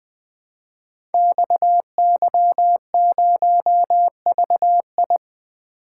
Morse code, heard as a keyed tone, about 20 wpm, 700 Hz. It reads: XY0VI
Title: Code Groups - 5 Letter:Numbers - 20wpm